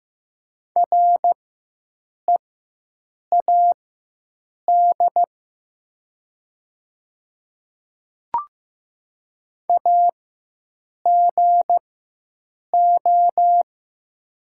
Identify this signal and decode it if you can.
Morse code, heard as a keyed tone, about 15 wpm, spaced as for 7 wpm, 700 Hz. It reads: READ AGO
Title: 4x Character Spacing - Top 500 Words - 15wpm